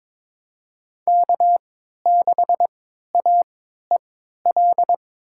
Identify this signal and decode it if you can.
Morse code, heard as a keyed tone, about 22 wpm, spaced as for 13 wpm, 700 Hz. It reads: K6AEL
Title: USA Call Signs - 3x Character Spacing - 22wpm